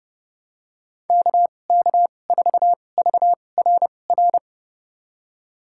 Morse code, heard as a keyed tone, about 32 wpm, spaced as for 22 wpm, 700 Hz. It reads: KK4VRR
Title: USA Call Signs - 6 Characters - 2x Character Spacing - 30wpm